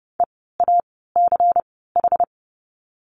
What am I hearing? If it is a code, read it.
Morse code, heard as a keyed tone, about 30 wpm, spaced as for 17 wpm, 700 Hz. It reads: EACH
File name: Lesson 16f - Letter c - Words 30wpm